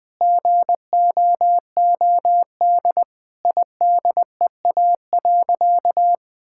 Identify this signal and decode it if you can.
Morse code, heard as a keyed tone, about 20 wpm, 700 Hz. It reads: GOOD IDEA.
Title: Sentences with 5-Word Phrases - 20wpm